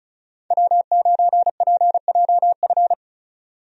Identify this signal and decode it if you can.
Morse code, heard as a keyed tone, about 34 wpm, 700 Hz. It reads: W9PJF
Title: USA Call Signs - 5 Characters - 35wpm